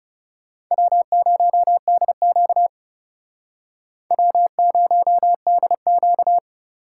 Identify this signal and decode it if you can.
Morse code, heard as a keyed tone, about 39 wpm, 700 Hz. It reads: W0DQ W0DQ